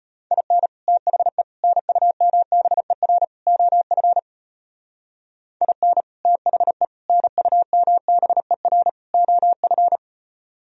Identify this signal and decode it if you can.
Morse code, heard as a keyed tone, about 32 wpm, 700 Hz. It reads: IN THE NUMBER OF IN THE NUMBER OF